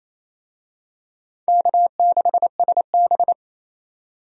Morse code, heard as a keyed tone, about 28 wpm, 700 Hz. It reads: K6SB
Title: USA Call Signs - 4 Characters - 28wpm